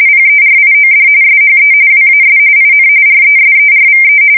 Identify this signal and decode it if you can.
RTTY, about 45.45 baud. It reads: PERIOD SHE WAS IN RESERVE
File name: fsk170 / USS-Massachusetts